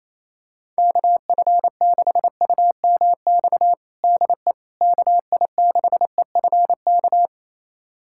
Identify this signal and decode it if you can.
Morse code, heard as a keyed tone, about 28 wpm, 700 Hz. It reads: KF6UMX DE KI6EFK
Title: USA Callsign DE Callsign - 6 Characters - 28wpm